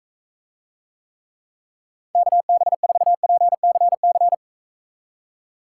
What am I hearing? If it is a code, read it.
Morse code, heard as a keyed tone, about 44 wpm, 700 Hz. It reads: KB4PCC